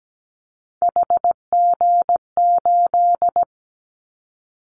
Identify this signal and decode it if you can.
Morse code, heard as a keyed tone, about 17 wpm, 700 Hz. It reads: HG8
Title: Callsign Prefixes for Top 50 DXCC - 17wpm